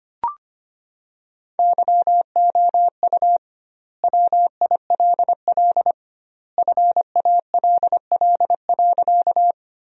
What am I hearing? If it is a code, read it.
Morse code, heard as a keyed tone, about 25 wpm, 700 Hz. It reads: YOU WILL FALL.
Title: 2x Word-Spacing - Sentences from Top 500 Words - 25wpm